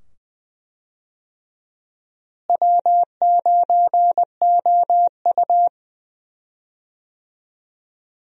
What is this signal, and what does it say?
Morse code, heard as a keyed tone, about 20 wpm, 700 Hz. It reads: W9OU
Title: USA Call Signs - 4 Characters - Repeated 2x - 20wpm